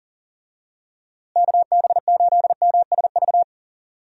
Morse code, heard as a keyed tone, about 40 wpm, 700 Hz. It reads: KB8MSV
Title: USA Call Signs - 6 Characters - Repeated 2x - 40wpm